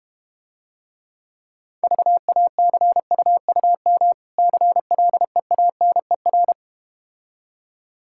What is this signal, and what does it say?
Morse code, heard as a keyed tone, about 32 wpm, 700 Hz. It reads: VACUUM CLEANER